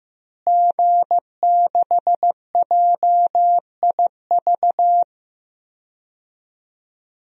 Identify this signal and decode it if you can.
Morse code, heard as a keyed tone, about 15 wpm, 700 Hz. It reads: G6JIV
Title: DXCC-100 Call Signs - 5 Characters - Repeated 3x - 15wpm